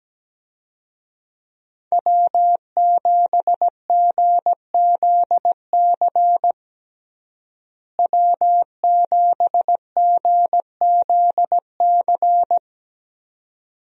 Morse code, heard as a keyed tone, about 17 wpm, 700 Hz. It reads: W7GZC W7GZC
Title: USA Call Signs - 5 Characters - Repeated 2x - 17wpm